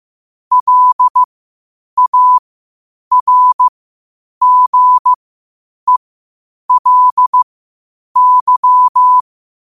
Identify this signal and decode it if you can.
Morse code, heard as a keyed tone, about 15 wpm, spaced as for 8 wpm, 1000 Hz. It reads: LARGELY
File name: qotdfiles / 8wpm